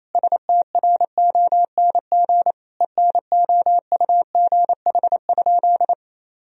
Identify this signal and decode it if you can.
Morse code, heard as a keyed tone, about 28 wpm, 700 Hz. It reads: STRONG ENOUGH?
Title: Sentences from Top 1000 Words - Review Part 3 - 28wpm